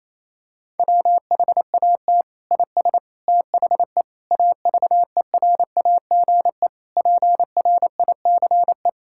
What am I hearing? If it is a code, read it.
Morse code, heard as a keyed tone, about 28 wpm, 700 Hz. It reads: WHAT IS THE AVERAGE PRICE
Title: Sentences from Top 1000 Words - Review Part 6 - 28wpm